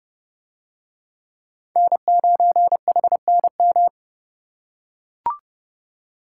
Morse code, heard as a keyed tone, about 28 wpm, 700 Hz. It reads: N9HNM